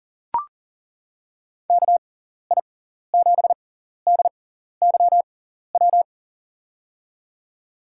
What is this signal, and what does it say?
Morse code, heard as a keyed tone, about 40 wpm, spaced as for 14 wpm, 700 Hz. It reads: KI7DYW
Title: USA Call Signs - 6 Characters - 6x Character Spacing - 40wpm